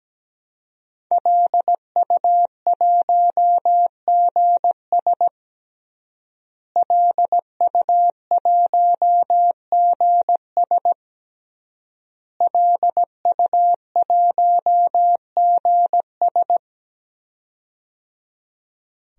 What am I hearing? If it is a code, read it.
Morse code, heard as a keyed tone, about 17 wpm, 700 Hz. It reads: LU1GS LU1GS LU1GS
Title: DXCC-100 Call Signs - 5 Characters - Repeated 3x - 17wpm